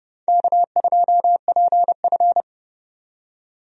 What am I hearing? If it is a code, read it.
Morse code, heard as a keyed tone, about 30 wpm, 700 Hz. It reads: K2PF
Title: USA Call Signs - 4 Characters - 30wpm